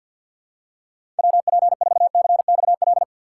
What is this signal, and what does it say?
Morse code, heard as a keyed tone, about 52 wpm, 700 Hz. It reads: WP4CXL
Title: USA Call Signs - 6 Characters - 3x Character Spacing - 50wpm